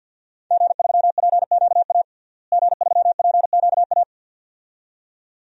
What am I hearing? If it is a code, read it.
Morse code, heard as a keyed tone, about 48 wpm, 700 Hz. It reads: G3PQA G3PQA
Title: DXCC-100 Call Signs - 5 Characters - Repeated 2x - 50wpm